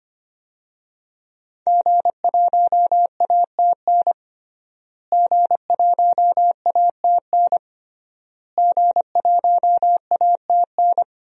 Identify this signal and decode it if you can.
Morse code, heard as a keyed tone, about 25 wpm, 700 Hz. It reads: G1ATN G1ATN G1ATN
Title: DXCC-100 Call Signs - 5 Characters - Repeated 3x - 25wpm